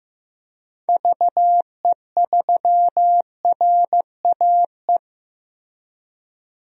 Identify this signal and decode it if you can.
Morse code, heard as a keyed tone, about 15 wpm, 700 Hz. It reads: VE3RAE
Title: DXCC-100 Call Signs - 6 Characters - Repeated 2x - 15wpm